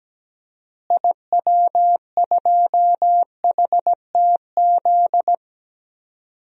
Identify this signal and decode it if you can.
Morse code, heard as a keyed tone, about 17 wpm, 700 Hz. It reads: IW2HTZ
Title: DXCC-100 Call Signs - 6 Characters - Repeated 3x - 17wpm